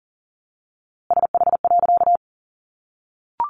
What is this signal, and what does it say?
Morse code, heard as a keyed tone, about 40 wpm, 700 Hz. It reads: SH.
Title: A Sound Of Thunder 40wpm